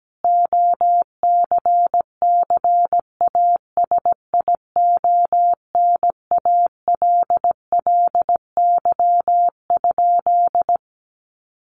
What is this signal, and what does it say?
Morse code, heard as a keyed tone, about 17 wpm, 700 Hz. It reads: OCCASIONALLY?
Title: Sentences from Top 1300 Words 17wpm